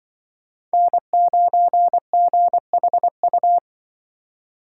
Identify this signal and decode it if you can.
Morse code, heard as a keyed tone, about 24 wpm, 700 Hz. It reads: N9GHU
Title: Speed Racing - USA Call Signs 20wpm